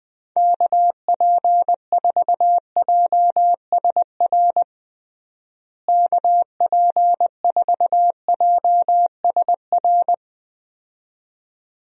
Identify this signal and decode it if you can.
Morse code, heard as a keyed tone, about 21 wpm, 700 Hz. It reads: KP4JSR KP4JSR
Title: USA Call Signs - 6 Characters - Repeated 2x - 20wpm